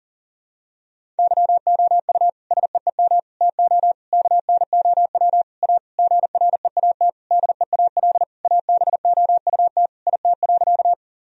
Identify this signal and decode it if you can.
Morse code, heard as a keyed tone, about 40 wpm, 700 Hz. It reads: YOU SEEM TO KNOW A GREAT DEAL ABOUT IT.
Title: Sentences from Top 1000 Words - Review Part 1 - 40wpm